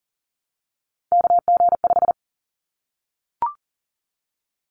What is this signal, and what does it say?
Morse code, heard as a keyed tone, about 40 wpm, 700 Hz. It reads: KG5
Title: Callsign Prefixes for Top 25 DXCC - 40wpm